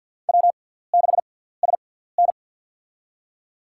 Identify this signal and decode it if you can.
Morse code, heard as a keyed tone, about 50 wpm, spaced as for 17 wpm, 690 Hz. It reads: W6SN
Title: USA Call Signs - 6x Character Spacing - 50wpm